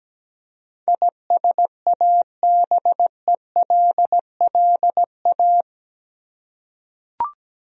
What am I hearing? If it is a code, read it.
Morse code, heard as a keyed tone, about 17 wpm, 700 Hz. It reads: ISABELLA
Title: USA Common Names - 17wpm